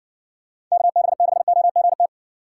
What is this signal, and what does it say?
Morse code, heard as a keyed tone, about 60 wpm, 700 Hz. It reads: KB6OGT